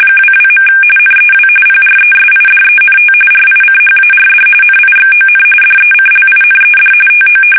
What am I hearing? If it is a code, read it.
RTTY, about 45.45 baud. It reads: 22. NEW PRESIDENT RICHARD NIXON DECLINED TO
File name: fsk850 / battleshipnj